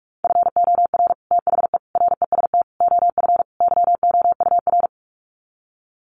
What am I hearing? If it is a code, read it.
Morse code, heard as a keyed tone, about 44 wpm, 700 Hz. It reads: FOR THE REST OF YOUR
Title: Sets of 5 Words Encore 45wpm